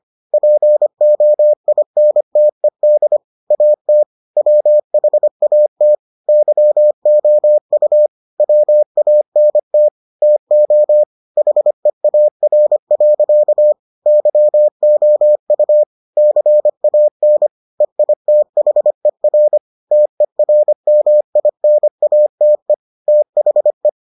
Morse code, heard as a keyed tone, about 25 wpm, 600 Hz. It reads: POINTED AT WHAT YOU WANT TO HEAR. YOU CAN EITHER TERMINATE THE